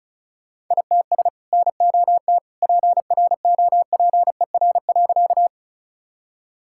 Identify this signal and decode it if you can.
Morse code, heard as a keyed tone, about 34 wpm, 700 Hz. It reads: ITS NOT PROPER.
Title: Sentences from Top 1000 Words - Review Part 5 - 35wpm